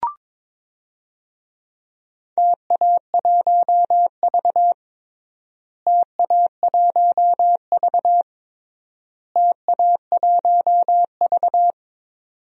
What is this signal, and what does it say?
Morse code, heard as a keyed tone, about 22 wpm, 700 Hz. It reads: TA1V TA1V TA1V
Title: DXCC-100 Call Signs - 4 Characters - Repeated 3x - 22wpm